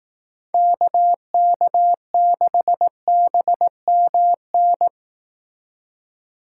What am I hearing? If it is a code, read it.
Morse code, heard as a keyed tone, about 18 wpm, 700 Hz. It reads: KK6BMN
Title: Speed Racing - USA 2x3 Call Signs 15wpm